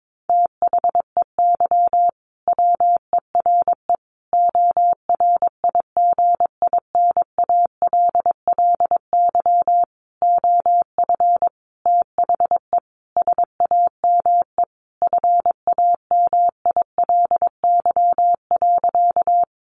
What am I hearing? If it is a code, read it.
Morse code, heard as a keyed tone, about 22 wpm, 700 Hz. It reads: THEY WERE ORIGINALLY OF THE SAME FAMILY.
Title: Sentences from Top 1500 Words 22wpm